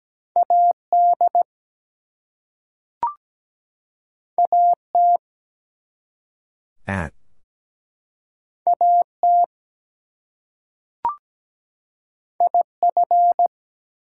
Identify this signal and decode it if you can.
Morse code, heard as a keyed tone, about 17 wpm, 700 Hz. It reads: AD AT AT IF
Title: Two Letter Words - 17wpm